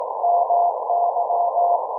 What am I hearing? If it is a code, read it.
Morse code, heard as a keyed tone, about 18 wpm, 690 Hz, with noise. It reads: MY